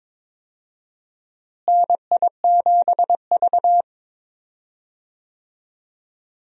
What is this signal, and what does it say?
Morse code, heard as a keyed tone, about 22 wpm, 700 Hz. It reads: NI7V
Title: USA Call Signs - 4 Characters - Repeated 3x - 22wpm